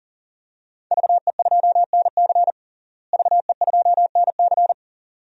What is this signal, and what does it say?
Morse code, heard as a keyed tone, about 40 wpm, 700 Hz. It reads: VE2NC VE2NC
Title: DXCC-100 Call Signs - 5 Characters - Repeated 2x - 40wpm